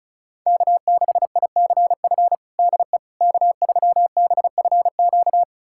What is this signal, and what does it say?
Morse code, heard as a keyed tone, about 36 wpm, 700 Hz. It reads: K6ICF DE K3BFQ
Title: USA Callsign DE Callsign - 5 Characters - 35wpm